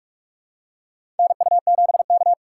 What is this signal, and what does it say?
Morse code, heard as a keyed tone, about 46 wpm, 700 Hz. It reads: NU7K
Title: USA Call Signs - 4 Characters - 3x Character Spacing - 45wpm